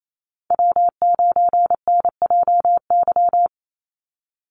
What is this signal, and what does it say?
Morse code, heard as a keyed tone, about 28 wpm, 700 Hz. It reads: W9NJY
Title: Call Signs - 28wpm